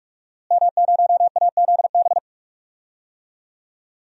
Morse code, heard as a keyed tone, about 44 wpm, 700 Hz. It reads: M0AZB
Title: DXCC-100 Call Signs - 5 Characters - Repeated 2x - 45wpm